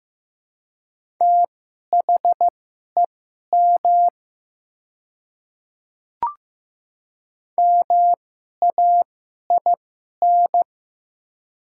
Morse code, heard as a keyed tone, about 15 wpm, spaced as for 11 wpm, 700 Hz. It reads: THEM MAIN
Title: 2x Character Spacing - Top 300 Words - 15wpm